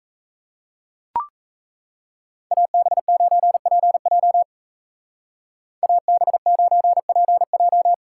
Morse code, heard as a keyed tone, about 34 wpm, 700 Hz. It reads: AB9PJ AB9PJ
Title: Speed Racing - USA 2x2 Call Signs 28wpm